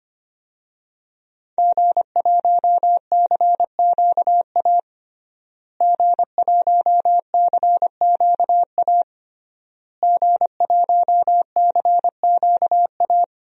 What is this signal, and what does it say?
Morse code, heard as a keyed tone, about 25 wpm, 700 Hz. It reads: G1CQA G1CQA G1CQA
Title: DXCC-100 Call Signs - 5 Characters - Repeated 3x - 25wpm